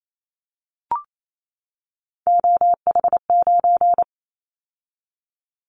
Morse code, heard as a keyed tone, about 28 wpm, 700 Hz. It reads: OH9
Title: Callsign Prefixes for Top 50 DXCC - 28wpm